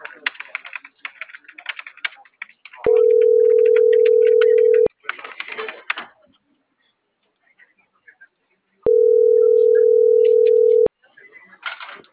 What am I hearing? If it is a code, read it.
Ringback tone (North America), 440 + 480 Hz.